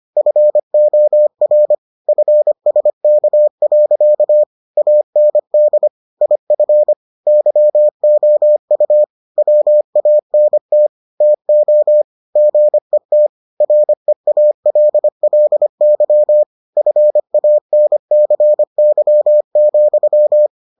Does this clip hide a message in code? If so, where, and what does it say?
Yes: Morse code from 0.2 to 20.5 s, heard as a keyed tone, about 25 wpm, 600 Hz. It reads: FOR FSK. AND IF YOU WANT TO GET REALLY FANCY,